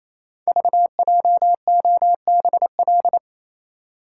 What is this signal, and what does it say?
Morse code, heard as a keyed tone, about 27 wpm, 700 Hz. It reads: VJOBL
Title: Code Groups - 5 Letters - 28wpm